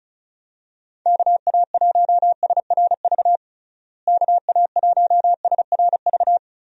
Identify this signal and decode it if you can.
Morse code, heard as a keyed tone, about 36 wpm, 700 Hz. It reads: KA1SRV KA1SRV
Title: USA Call Signs - 6 Characters - Repeated 3x - 35wpm